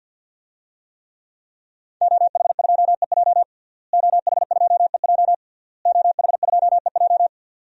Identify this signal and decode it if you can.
Morse code, heard as a keyed tone, about 52 wpm, 700 Hz. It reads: OH2EJ OH2EJ OH2EJ
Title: DXCC-100 Call Signs - 5 Characters - Repeated 3x - 50wpm